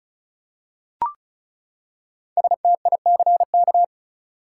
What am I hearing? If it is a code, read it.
Morse code, heard as a keyed tone, about 34 wpm, 700 Hz. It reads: STICK
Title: Five Letter Words - 35wpm